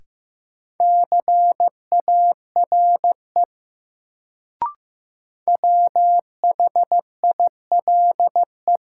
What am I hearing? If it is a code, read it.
Morse code, heard as a keyed tone, about 15 wpm, 700 Hz. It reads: CARE WHILE
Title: Sets of 3 Words - New Word Drill - 15wpm